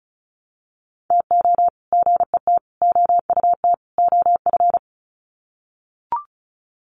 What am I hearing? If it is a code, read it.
Morse code, heard as a keyed tone, about 34 wpm, 700 Hz. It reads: TO GET OUT OF
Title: Sets of 4 Words 35wpm